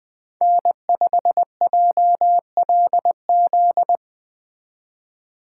Morse code, heard as a keyed tone, about 20 wpm, 700 Hz. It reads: N5JLZ